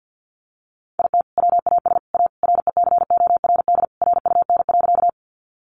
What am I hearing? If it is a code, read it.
Morse code, heard as a keyed tone, about 50 wpm, 700 Hz. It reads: IT WAS A RECORD RUN.
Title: Sentences from Top 500 Words 50wpm